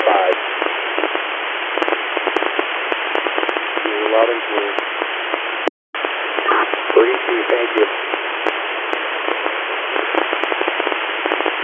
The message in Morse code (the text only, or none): none